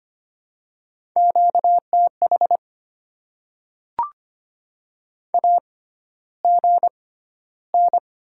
Morse code, heard as a keyed tone, about 25 wpm, spaced as for 13 wpm, 690 Hz. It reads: QTH AGN
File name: Top 20 QSO Elements - 6x Character Spacing - 25wpm